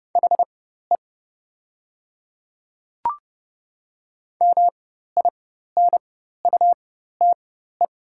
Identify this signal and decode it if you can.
Morse code, heard as a keyed tone, about 30 wpm, spaced as for 14 wpm, 690 Hz. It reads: HE MINUTE
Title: 4x Character Spacing - Top 500 Words - 30wpm